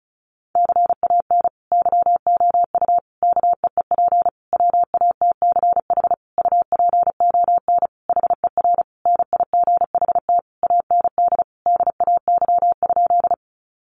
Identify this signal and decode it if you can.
Morse code, heard as a keyed tone, about 34 wpm, 700 Hz. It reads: CAN YOU KEEP WATCH UPON HER NIGHT AND DAY?
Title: Sentences from Top 1100 Words 35wpm